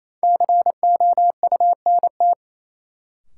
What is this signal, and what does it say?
Morse code, heard as a keyed tone, about 28 wpm, 700 Hz. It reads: COUNT
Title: Sentences from Top 1500 Words - New Word Drill - 28wpm